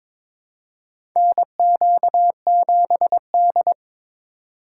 Morse code, heard as a keyed tone, about 22 wpm, 700 Hz. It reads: NQ7D
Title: USA Call Signs - 4 Characters - 4x Character Spacing - 22wpm